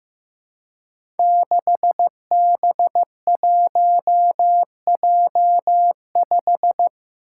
Morse code, heard as a keyed tone, about 15 wpm, 700 Hz. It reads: 6B1J5
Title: Five Challenging Letter-Number Combinations 15wpm